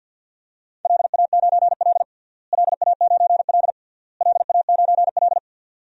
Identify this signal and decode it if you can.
Morse code, heard as a keyed tone, about 52 wpm, 700 Hz. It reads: LA9L LA9L LA9L
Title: DXCC-100 Call Signs - 4 Characters - Repeated 3x - 50wpm